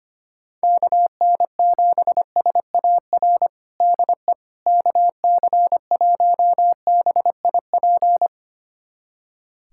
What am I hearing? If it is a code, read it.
Morse code, heard as a keyed tone, about 25 wpm, 700 Hz. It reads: KN7SAR DE KC1BIP
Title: USA Callsign DE Callsign 25wpm